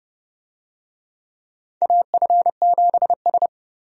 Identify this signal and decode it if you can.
Morse code, heard as a keyed tone, about 30 wpm, 700 Hz. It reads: AF7S
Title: USA Call Signs - 4 Characters - 30wpm